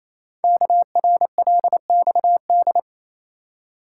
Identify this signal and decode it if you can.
Morse code, heard as a keyed tone, about 27 wpm, 700 Hz. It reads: KRLXD
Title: Code Groups - 5 Letters - 28wpm